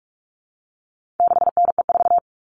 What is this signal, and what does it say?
Morse code, heard as a keyed tone, about 44 wpm, 700 Hz. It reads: 6NE4
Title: All Cut Numbers - 4-Digit 45wpm